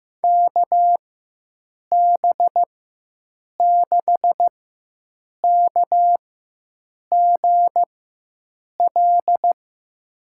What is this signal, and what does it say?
Morse code, heard as a keyed tone, about 15 wpm, spaced as for 7 wpm, 700 Hz. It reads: KB6KGL
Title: USA Call Signs - 6 Characters - 4x Character Spacing - 15wpm